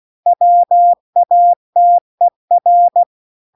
Morse code, heard as a keyed tone, about 16 wpm, 700 Hz. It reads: WATER